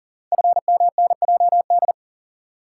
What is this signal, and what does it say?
Morse code, heard as a keyed tone, about 39 wpm, 700 Hz. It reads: FMNJD
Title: Code Groups - 5 Letters - 40wpm